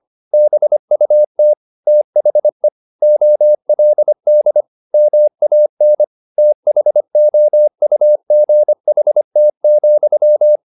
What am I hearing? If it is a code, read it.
Morse code, heard as a keyed tone, about 25 wpm, 600 Hz. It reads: BUT THE OLD MAN THOUGHT,
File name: the old man and the sea part 1 25wpm